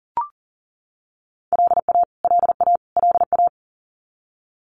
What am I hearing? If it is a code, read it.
Morse code, heard as a keyed tone, about 40 wpm, 700 Hz. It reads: LA LA LA
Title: US States Repeated 3x 40wpm